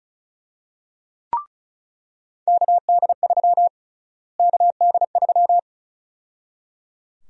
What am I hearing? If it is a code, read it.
Morse code, heard as a keyed tone, about 34 wpm, 700 Hz. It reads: KD3 KD3
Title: USA Call Sign Prefixes - Repeated 2x - 35wpm